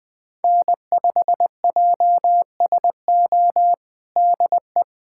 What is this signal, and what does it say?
Morse code, heard as a keyed tone, about 20 wpm, 700 Hz. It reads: N5JSO DE
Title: USA Callsign DE Callsign - 5 Characters - 20wpm